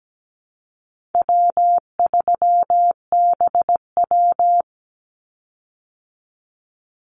Morse code, heard as a keyed tone, about 17 wpm, 700 Hz. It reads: W3BW
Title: Easy Call Signs Repeated 3x - 17wpm